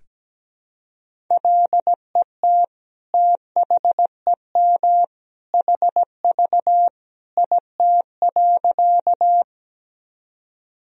Morse code, heard as a keyed tone, about 17 wpm, 700 Hz. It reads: LET THEM HV IT.
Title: Sentences from Top 300 Words and Abbreviations 17wpm